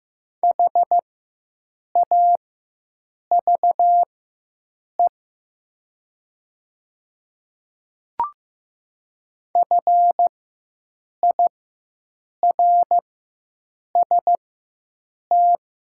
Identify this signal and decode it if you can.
Morse code, heard as a keyed tone, about 15 wpm, spaced as for 7 wpm, 700 Hz. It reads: HAVE FIRST
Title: 4x Character Spacing - Top 200 Words - 15wpm